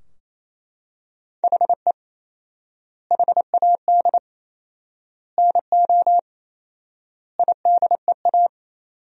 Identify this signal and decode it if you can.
Morse code, heard as a keyed tone, about 28 wpm, 700 Hz. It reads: HE HAD NO IDEA
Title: 4x Word-Spacing - Sentences from Top 300 Words - 28wpm